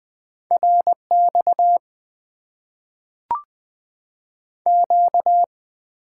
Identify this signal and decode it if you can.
Morse code, heard as a keyed tone, about 20 wpm, 700 Hz. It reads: RX Q
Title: Top 90 QSO Elements - 8x Character Spacing - 20wpm